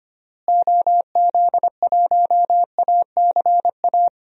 Morse code, heard as a keyed tone, about 25 wpm, 700 Hz. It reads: OZ1ACA
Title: DXCC-100 Call Signs - 6 Characters - Repeated 3x - 25wpm